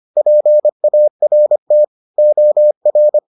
Morse code, heard as a keyed tone, about 25 wpm, 600 Hz. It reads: PART OR